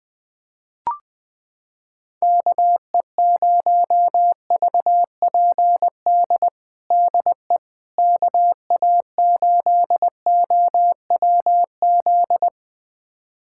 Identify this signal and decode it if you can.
Morse code, heard as a keyed tone, about 20 wpm, 700 Hz. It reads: KE0VPD DE KA8OWZ
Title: USA Callsign DE Callsign - 6 Characters - 20wpm